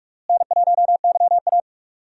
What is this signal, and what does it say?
Morse code, heard as a keyed tone, about 41 wpm, 700 Hz. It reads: N1YA
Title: USA Call Signs - 4 Characters - Repeated 3x - 45wpm